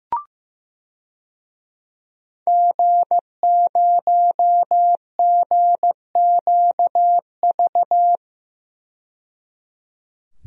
Morse code, heard as a keyed tone, about 15 wpm, 700 Hz. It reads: G0GQV
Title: DXCC-100 Call Signs - 5 Characters - 15wpm